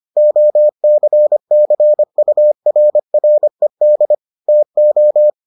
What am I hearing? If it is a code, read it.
Morse code, heard as a keyed tone, about 25 wpm, 600 Hz. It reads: OCCURRED TO